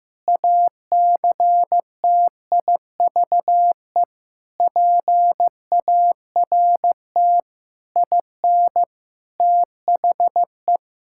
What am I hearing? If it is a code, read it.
Morse code, heard as a keyed tone, about 15 wpm, 700 Hz. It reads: ACTIVE PART IN THE
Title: Sentences from Top 1000 Words - Review Part 6 - 15wpm